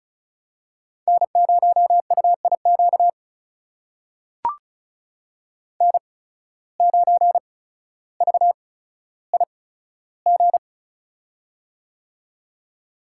Morse code, heard as a keyed tone, about 36 wpm, spaced as for 15 wpm, 700 Hz. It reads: N0UIQ N9VIG
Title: USA Call Signs - 5 Characters - 8x Character Spacing - 35wpm